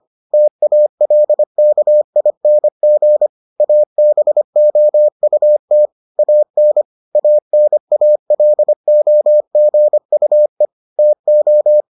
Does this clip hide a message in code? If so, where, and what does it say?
Yes: Morse code from 0.3 to 11.8 s, heard as a keyed tone, about 25 wpm, 600 Hz. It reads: TALKING ABOUT AN ANALOGUE TO